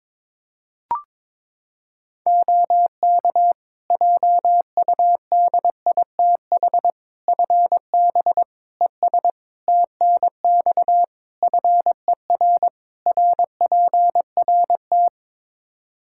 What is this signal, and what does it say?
Morse code, heard as a keyed tone, about 22 wpm, 700 Hz. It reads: OK JUDITH FB ES TNX FER RPRT
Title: Practice QSO - Exchange 2 - Info - 22wpm